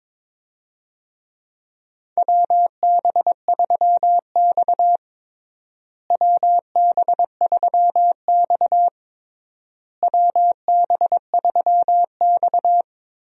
Morse code, heard as a keyed tone, about 22 wpm, 700 Hz. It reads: WB3X WB3X WB3X
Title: USA Call Signs - 4 Characters - Repeated 3x - 22wpm